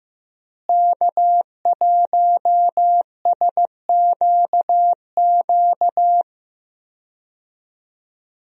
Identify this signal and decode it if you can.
Morse code, heard as a keyed tone, about 15 wpm, 700 Hz. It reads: K1SQQ
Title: USA Call Signs - 5 Characters - Repeated 3x - 15wpm